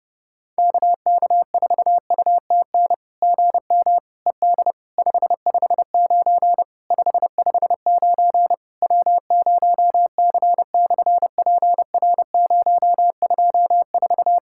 Morse code, heard as a keyed tone, about 30 wpm, 700 Hz. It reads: KK4UTN GM ED 559 559 W0C/PR024
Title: SOTA QSO Practice 30wpm